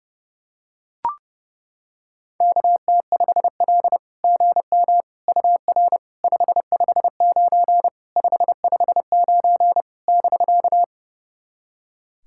Morse code, heard as a keyed tone, about 30 wpm, 700 Hz. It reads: KT5L GM UR 559 559 <BK>
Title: SOTA QSO Practice 30wpm